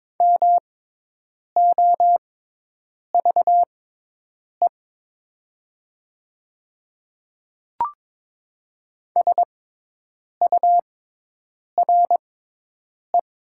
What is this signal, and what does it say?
Morse code, heard as a keyed tone, about 22 wpm, spaced as for 8 wpm, 700 Hz. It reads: MOVE SURE
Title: 6x Character Spacing - Top 500 Words - 22wpm